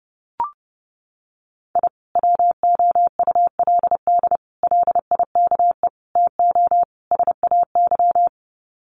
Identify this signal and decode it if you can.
Morse code, heard as a keyed tone, about 30 wpm, 700 Hz. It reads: I WOULD LIKE TO SAY
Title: Sets of 5 Words Encore2 30wpm